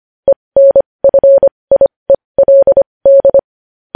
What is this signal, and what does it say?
Morse code, heard as a keyed tone, about 25 wpm, spaced as for 20 wpm, 550 Hz. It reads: ENFIELD
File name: podcast / News headlines-1x-25x20